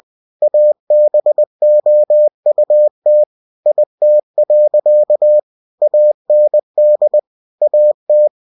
Morse code, heard as a keyed tone, about 20 wpm, 600 Hz. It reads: ABOUT IT. AND AT